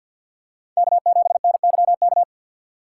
Morse code, heard as a keyed tone, about 52 wpm, 700 Hz. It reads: K7NYK
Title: USA Call Signs - 5 Characters - 3x Character Spacing - 50wpm